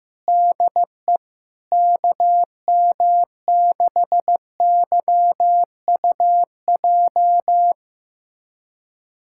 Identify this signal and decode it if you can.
Morse code, heard as a keyed tone, about 15 wpm, 700 Hz. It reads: DE KM6YUJ
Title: USA Callsign DE Callsign 15wpm